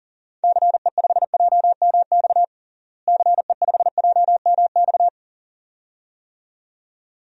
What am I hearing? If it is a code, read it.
Morse code, heard as a keyed tone, about 40 wpm, 700 Hz. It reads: CE5JMX CE5JMX
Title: DXCC-100 Call Signs - 6 Characters - Repeated 2x - 40wpm